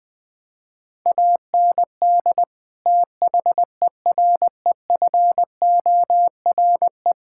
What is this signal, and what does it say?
Morse code, heard as a keyed tone, about 20 wpm, 700 Hz. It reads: AND THEREFORE
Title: Sentences from Top 1000 Words - Review Part 5 - 20wpm